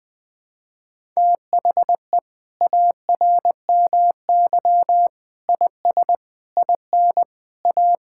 Morse code, heard as a keyed tone, about 20 wpm, 700 Hz. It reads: THE ARMY IS IN A
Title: Sentences from Top 1000 Words - Review Part 6 - 20wpm